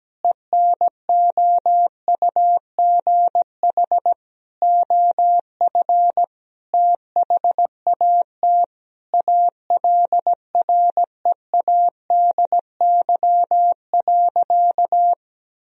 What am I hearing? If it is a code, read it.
Morse code, heard as a keyed tone, about 17 wpm, 700 Hz. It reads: ENOUGH OF THAT ALREADY.
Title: Sentences from Top 1000 Words - Review Part 3 - 17wpm